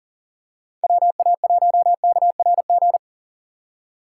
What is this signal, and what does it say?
Morse code, heard as a keyed tone, about 38 wpm, 700 Hz. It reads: WA1KRG
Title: Speed Racing - USA 2x3 Call Signs 40wpm